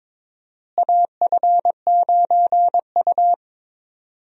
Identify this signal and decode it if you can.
Morse code, heard as a keyed tone, about 22 wpm, 700 Hz. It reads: AF9U
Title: USA Call Signs - 4 Characters - Repeated 2x - 22wpm